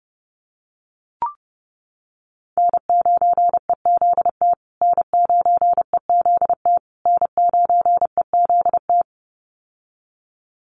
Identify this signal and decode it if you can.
Morse code, heard as a keyed tone, about 30 wpm, 700 Hz. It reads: N9EZT N9EZT N9EZT
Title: USA Call Signs Repeated 3x 30wpm